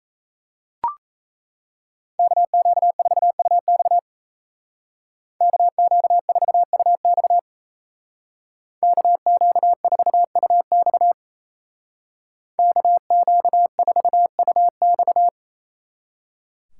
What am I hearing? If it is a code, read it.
Morse code, heard as a keyed tone, about 34 wpm, 700 Hz. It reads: KQ4UX KQ4UX KQ4UX KQ4UX
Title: Speed Racing - USA 2x2 Call Signs 28wpm